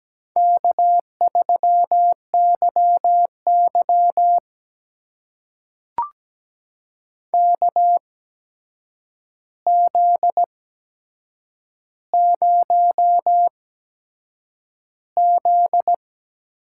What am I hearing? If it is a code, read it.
Morse code, heard as a keyed tone, about 17 wpm, spaced as for 7 wpm, 700 Hz. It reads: K3YY KZ0Z
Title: USA Call Signs - 4 Characters - 8x Character Spacing - 17wpm